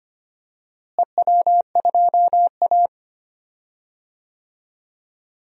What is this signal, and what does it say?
Morse code, heard as a keyed tone, about 25 wpm, 700 Hz. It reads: EW2A
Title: DXCC-100 Call Signs - 4 Characters - Repeated 2x - 25wpm